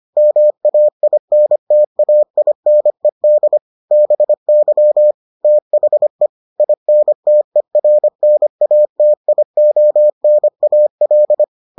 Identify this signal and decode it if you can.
Morse code, heard as a keyed tone, about 25 wpm, 600 Hz. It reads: MAINTAINED BY THE INTERNATIONAL